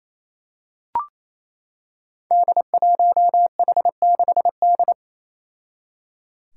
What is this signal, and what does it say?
Morse code, heard as a keyed tone, about 28 wpm, 700 Hz. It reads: D1H6D
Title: Five Challenging Letter-Number Combinations 28wpm